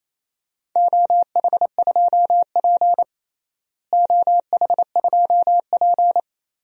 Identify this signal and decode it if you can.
Morse code, heard as a keyed tone, about 28 wpm, 700 Hz. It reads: OH2P OH2P
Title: DXCC-100 Call Signs - 4 Characters - Repeated 3x - 28wpm